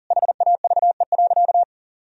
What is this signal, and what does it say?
Morse code, heard as a keyed tone, about 40 wpm, 700 Hz. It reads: HAVE.
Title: Sentences from Top 1000 Words - Review Part 4 - 40wpm